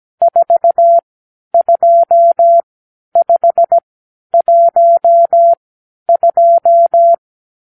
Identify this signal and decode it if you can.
Morse code, heard as a keyed tone, about 17 wpm, spaced as for 10 wpm, 680 Hz. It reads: 42512
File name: k7qo / Lesson 50 (1-6)